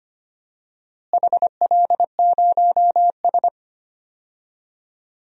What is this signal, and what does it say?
Morse code, heard as a keyed tone, about 25 wpm, 700 Hz. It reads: HL0S
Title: DXCC-100 Call Signs - 4 Characters - Repeated 2x - 25wpm